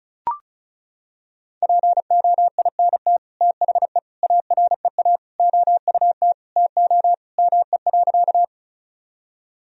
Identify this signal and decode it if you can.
Morse code, heard as a keyed tone, about 36 wpm, 700 Hz. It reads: POINT THE AREA OUT TO ME.
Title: Sentences from Top 1000 Words - Review Part 2 - 35wpm